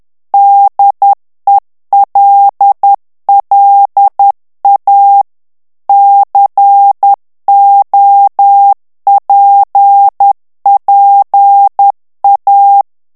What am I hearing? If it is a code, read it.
Morse code, heard as a keyed tone, about 10 wpm, 800 Hz. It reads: DELLA COPPA
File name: ITALIEN texte clair 2 vitesse 10 mots minute